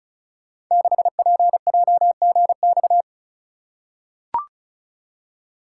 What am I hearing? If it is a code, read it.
Morse code, heard as a keyed tone, about 36 wpm, 690 Hz. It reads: 6PJGX
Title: Code Groups - 5 Letter:Numbers - 35wpm